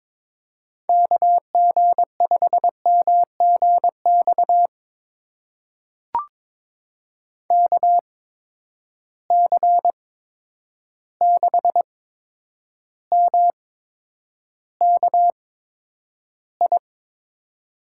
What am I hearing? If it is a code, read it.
Morse code, heard as a keyed tone, about 22 wpm, spaced as for 9 wpm, 700 Hz. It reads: KG5MGX KC6MKI
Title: USA Call Signs - 6 Characters - 8x Character Spacing - 22wpm